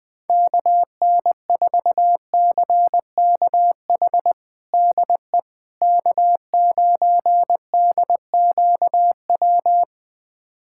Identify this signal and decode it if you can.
Morse code, heard as a keyed tone, about 20 wpm, 700 Hz. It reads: KN4CKH DE K9DQW
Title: USA Callsign DE Callsign 20wpm